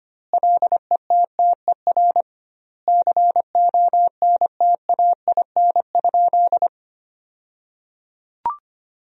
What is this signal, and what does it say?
Morse code, heard as a keyed tone, about 25 wpm, 700 Hz. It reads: LETTER CONTAIN?
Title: 2x Word-Spacing - Sentences from Top 500 Words - 25wpm